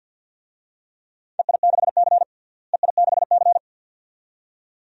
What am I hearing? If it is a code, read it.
Morse code, heard as a keyed tone, about 53 wpm, 700 Hz. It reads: EI6C EI6C
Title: DXCC-100 Call Signs - 4 Characters - Repeated 2x - 50wpm